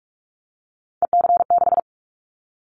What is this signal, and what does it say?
Morse code, heard as a keyed tone, about 44 wpm, 700 Hz. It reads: EC6
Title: Callsign Prefixes for Top 100 DXCC - 45wpm